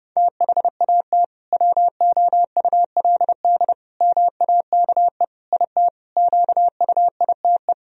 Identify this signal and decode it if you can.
Morse code, heard as a keyed tone, about 30 wpm, 700 Hz. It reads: THAT WOULD MAKE IT QUITE
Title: Sentences from Top 1000 Words - Review Part 2 - 30wpm